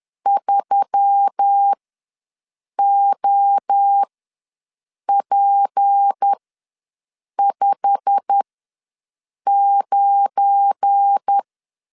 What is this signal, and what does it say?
Morse code, heard as a keyed tone, about 10 wpm, spaced as for 6 wpm, 800 Hz. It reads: 3OP59
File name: LEZIONI / 8 YJ45 / 8 YJ45 ws 10wpm 3843